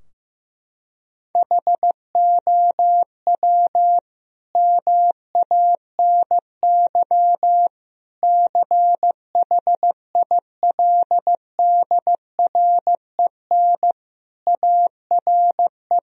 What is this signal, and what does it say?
Morse code, heard as a keyed tone, about 16 wpm, 700 Hz. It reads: HOW MANY CHILDREN ARE
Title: Sentences from Top 1000 Words - Review Part 4 - 15wpm